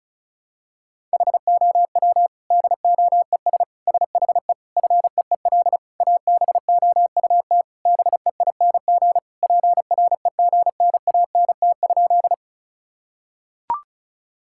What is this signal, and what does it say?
Morse code, heard as a keyed tone, about 36 wpm, 700 Hz. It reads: HOW DOES SHE FEEL ABOUT BEING PREGNANT?
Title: Sentences from Top 1000 Words - Review Part 6 - 35wpm